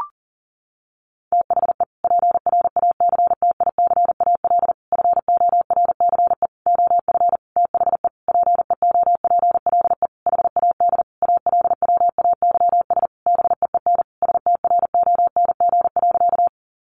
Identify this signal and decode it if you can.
Morse code, heard as a keyed tone, about 40 wpm, 700 Hz. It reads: THE PRACTICAL FORCE OF THE PEOPLE HAD ALWAYS BEEN STRONG.
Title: Sentences from Top 1400 Words 40wpm